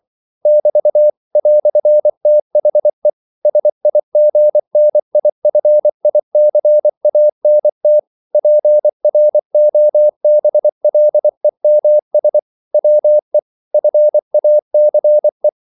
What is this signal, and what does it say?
Morse code, heard as a keyed tone, about 24 wpm, 600 Hz. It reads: <BT> "THE SIGNIFICANT PROBLEMS WE FACE
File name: CW / Practice / EnglishQuotations0224